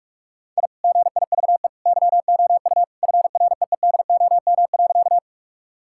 Morse code, heard as a keyed tone, about 41 wpm, 700 Hz. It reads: I GIVE YOU FREEDOM.
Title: Sentences from Top 1000 Words - Review Part 5 - 45wpm